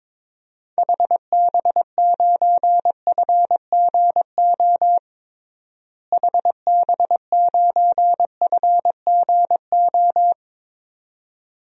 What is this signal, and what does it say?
Morse code, heard as a keyed tone, about 22 wpm, 700 Hz. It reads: HB9FGO HB9FGO
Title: DXCC-100 Call Signs - 6 Characters - Repeated 2x - 22wpm